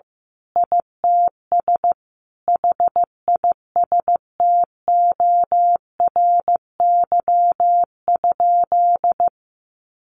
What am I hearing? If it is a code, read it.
Morse code, heard as a keyed tone, about 15 wpm, 700 Hz. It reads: ITS HISTORY?
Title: Sentences from Top 1100 Words 15wpm